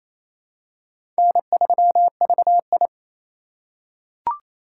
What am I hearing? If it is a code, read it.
Morse code, heard as a keyed tone, about 28 wpm, 700 Hz. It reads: N3VI
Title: USA Call Signs - 4 Characters - 6x Character Spacing - 28wpm